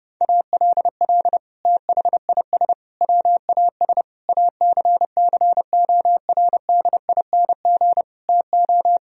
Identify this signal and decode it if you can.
Morse code, heard as a keyed tone, about 30 wpm, 700 Hz. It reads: ALL THIS WAS ACCORDING TO
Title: Sentences from Top 1000 Words - Review Part 4 - 30wpm